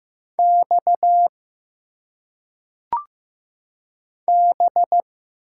Morse code, heard as a keyed tone, about 15 wpm, 700 Hz. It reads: X B
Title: Single Letters - Phonetic Pronunciation 15wpm